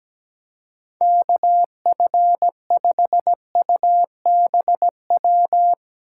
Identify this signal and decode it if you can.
Morse code, heard as a keyed tone, about 17 wpm, 700 Hz. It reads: KF5UBW
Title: Speed Racing - USA 2x3 Call Signs 17wpm